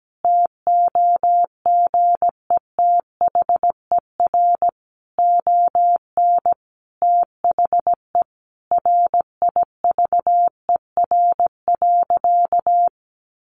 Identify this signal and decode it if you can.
Morse code, heard as a keyed tone, about 17 wpm, 700 Hz. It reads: TOGETHER ON THE RIVER.
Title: Sentences from Top 1300 Words 17wpm